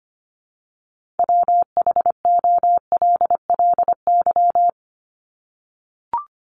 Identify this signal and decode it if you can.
Morse code, heard as a keyed tone, about 25 wpm, 700 Hz. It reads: WHOLLY
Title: LY Suffix - Top 100 Words with Base Words 25wpm